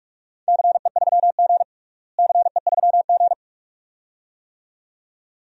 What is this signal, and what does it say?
Morse code, heard as a keyed tone, about 44 wpm, 700 Hz. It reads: CE3G CE3G
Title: DXCC-100 Call Signs - 4 Characters - Repeated 2x - 45wpm